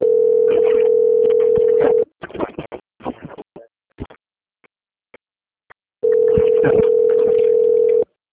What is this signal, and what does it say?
Ringback tone (North America), 440 + 480 Hz.